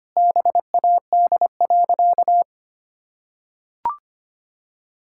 Morse code, heard as a keyed tone, about 25 wpm, 700 Hz. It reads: BAD.
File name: practice / 25wpm / Sentences with 2-Word Phrases - 25wpm